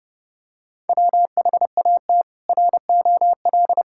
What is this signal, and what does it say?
Morse code, heard as a keyed tone, about 30 wpm, 700 Hz. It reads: WHAT ROL
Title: Sentences from Top 1000 Words - Review Part 6 - 30wpm